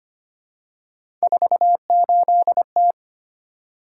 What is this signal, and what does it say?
Morse code, heard as a keyed tone, about 25 wpm, 700 Hz. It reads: 48T
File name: Three Letter-Number Combinations - 25wpm